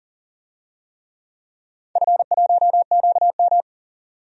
Morse code, heard as a keyed tone, about 40 wpm, 700 Hz. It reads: F1QM
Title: DXCC-100 Call Signs - 4 Characters - 40wpm